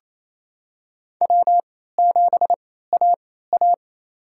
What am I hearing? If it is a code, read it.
Morse code, heard as a keyed tone, about 28 wpm, spaced as for 16 wpm, 700 Hz. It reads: W7AA
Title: USA Call Signs - 4 Characters - 3x Character Spacing - 28wpm